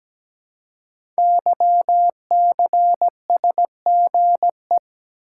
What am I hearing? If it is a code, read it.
Morse code, heard as a keyed tone, about 17 wpm, 700 Hz. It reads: YCSGE
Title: Code Groups - 5 Letters - 17wpm